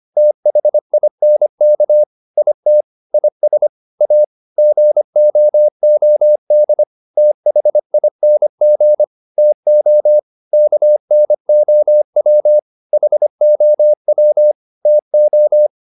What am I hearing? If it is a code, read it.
Morse code, heard as a keyed tone, about 25 wpm, 600 Hz. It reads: THINK IT IS A GOOD THING TO KNOW HOW TO